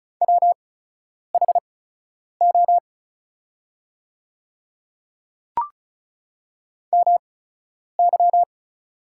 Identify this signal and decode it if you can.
Morse code, heard as a keyed tone, about 36 wpm, spaced as for 10 wpm, 700 Hz. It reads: WHO MY
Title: 8x Character Spacing - Top 100 Words - 35wpm